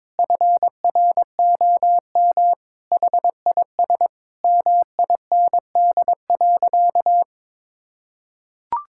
Morse code, heard as a keyed tone, about 22 wpm, 700 Hz. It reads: FROM HIS MIND.
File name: practice / 22wpm / Sentences with 2-Word Phrases - 22wpm